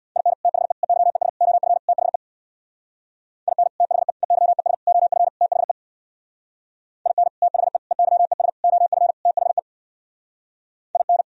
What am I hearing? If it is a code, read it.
Morse code, heard as a keyed tone, about 62 wpm, 700 Hz. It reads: IN THE EYES OF THE IN THE EYES OF THE IN THE EYES OF THE IN
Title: Speed-Racing - Sets of 5 Words 50wpm